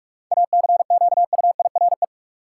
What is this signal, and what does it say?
Morse code, heard as a keyed tone, about 46 wpm, 700 Hz. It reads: ACQUIRE
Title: Seven Letter Words - 45wpm